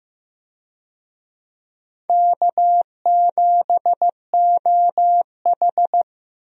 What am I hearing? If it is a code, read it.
Morse code, heard as a keyed tone, about 15 wpm, 700 Hz. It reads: K7OH
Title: USA Call Signs - 4 Characters - Repeated 3x - 15wpm